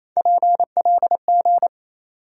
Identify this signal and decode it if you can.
Morse code, heard as a keyed tone, about 27 wpm, 700 Hz. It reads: PLG
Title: USA Call Sign Postfixes - 3 Letters 28wpm